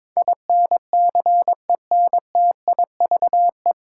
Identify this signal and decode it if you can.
Morse code, heard as a keyed tone, about 22 wpm, 700 Hz. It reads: INCENTIVE
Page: Morse Code Ninja practice 22wpm